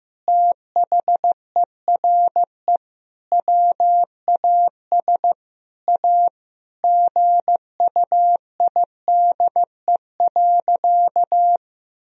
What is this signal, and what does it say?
Morse code, heard as a keyed tone, about 15 wpm, 700 Hz. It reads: THERE WAS A GUIDE.
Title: Sentences from Top 1000 Words - Review Part 3 - 15wpm